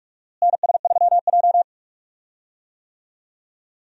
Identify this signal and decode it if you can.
Morse code, heard as a keyed tone, about 47 wpm, 700 Hz. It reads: NS3J
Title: USA Call Signs - 4 Characters - Repeated 3x - 45wpm